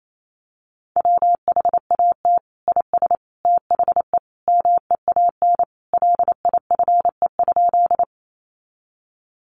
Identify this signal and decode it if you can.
Morse code, heard as a keyed tone, about 28 wpm, 700 Hz. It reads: WHAT IS THE MEAN LIFE?
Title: Sentences from Top 900 Words 28wpm